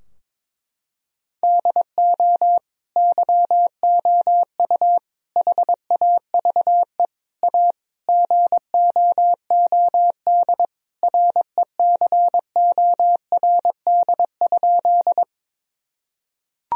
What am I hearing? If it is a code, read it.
Morse code, heard as a keyed tone, about 22 wpm, 700 Hz. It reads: DO YOU HAVE A GOOD RECORD?
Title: Sentences from Top 1000 Words - Review Part 3 - 22wpm